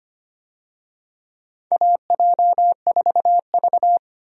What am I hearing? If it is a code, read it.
Morse code, heard as a keyed tone, about 25 wpm, 700 Hz. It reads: AJ4V
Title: USA Call Signs - 4 Characters - Repeated 3x - 25wpm